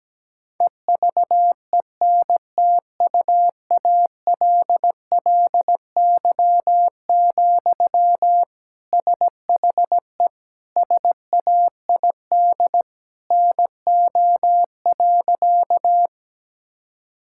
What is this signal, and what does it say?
Morse code, heard as a keyed tone, about 17 wpm, 700 Hz. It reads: EVENTUALLY, SHE SAID NO.
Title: Sentences from Top 1000 Words - Review Part 6 - 17wpm